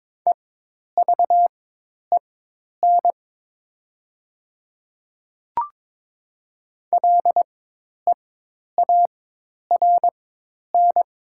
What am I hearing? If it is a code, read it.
Morse code, heard as a keyed tone, about 22 wpm, spaced as for 10 wpm, 700 Hz. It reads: EVEN LEARN
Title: 4x Character Spacing - Top 300 Words - 22wpm